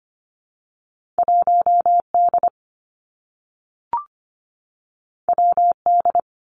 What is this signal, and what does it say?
Morse code, heard as a keyed tone, about 25 wpm, 700 Hz. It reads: 1 D W D
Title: Two Challenging Letter-Number Combinations 25wpm